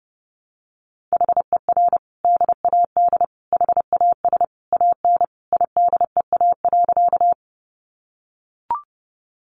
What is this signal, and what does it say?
Morse code, heard as a keyed tone, about 30 wpm, 700 Hz. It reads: HER DAD HAS AN IDEA.
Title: Additional Easy Phrases 30wpm